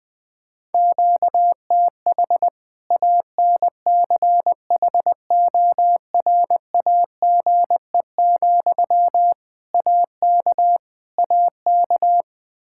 Morse code, heard as a keyed tone, about 21 wpm, 700 Hz. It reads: QTH ANCHORAGE, AK AK
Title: Practice QSO - Exchange 1 - Beginning - 20wpm